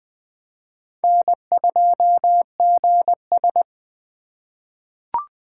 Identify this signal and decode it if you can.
Morse code, heard as a keyed tone, about 21 wpm, 700 Hz. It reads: N2GS
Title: USA Call Signs - 3x Character Spacing - 20wpm